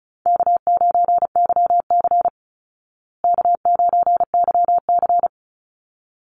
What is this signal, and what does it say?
Morse code, heard as a keyed tone, about 34 wpm, 700 Hz. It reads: K9YC K9YC
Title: Call Signs Repeated 3x - 35wpm